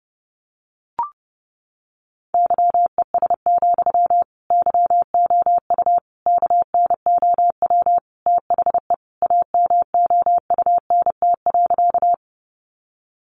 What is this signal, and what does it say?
Morse code, heard as a keyed tone, about 30 wpm, 700 Hz. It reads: YES, YOU KNOW THE AMOUNT.
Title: Sentences from Top 700 Words 30wpm